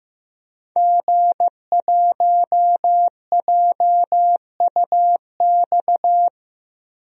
Morse code, heard as a keyed tone, about 15 wpm, 700 Hz. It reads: G1JUX
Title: DXCC-100 Call Signs - 5 Characters - 15wpm